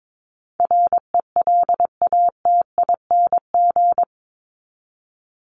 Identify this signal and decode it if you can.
Morse code, heard as a keyed tone, about 22 wpm, 700 Hz. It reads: RELATING
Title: ING Suffix - Top 100 Words with Base Words 22wpm